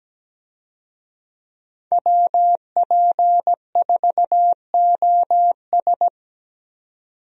Morse code, heard as a keyed tone, about 17 wpm, 700 Hz. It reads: WP4OS
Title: USA Call Signs - 5 Characters - 17wpm